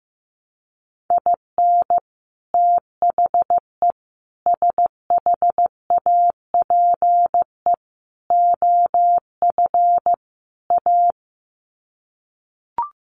Morse code, heard as a keyed tone, about 15 wpm, 700 Hz. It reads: IN THE SHAPE OF A
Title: Sets of 5 Words Encore 15wpm